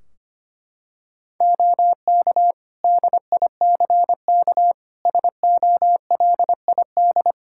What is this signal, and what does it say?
Morse code, heard as a keyed tone, about 25 wpm, 700 Hz. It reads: OK DICK SOLID
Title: Practice QSO - Exchange 3 - More Info - 25wpm